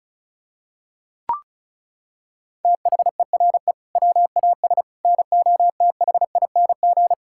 Morse code, heard as a keyed tone, about 36 wpm, 700 Hz. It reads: THERE WAS NOTHING
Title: Sentences from Top 1000 Words - Review Part 5 - 35wpm